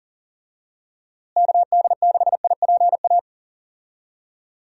Morse code, heard as a keyed tone, about 40 wpm, 700 Hz. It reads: KD6IPA
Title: USA Call Signs - 6 Characters - Repeated 2x - 40wpm